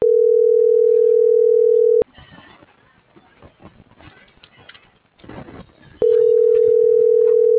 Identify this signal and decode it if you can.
Ringback tone (North America), 440 + 480 Hz.